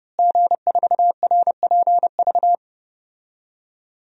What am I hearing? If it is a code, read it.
Morse code, heard as a keyed tone, about 32 wpm, 700 Hz. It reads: G4RPV
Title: DXCC-100 Call Signs - 5 Characters - Repeated 3x - 30wpm